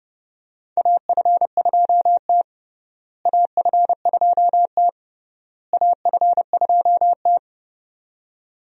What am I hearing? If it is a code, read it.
Morse code, heard as a keyed tone, about 30 wpm, 700 Hz. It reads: AF2T AF2T AF2T
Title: USA Call Signs - 4 Characters - Repeated 3x - 30wpm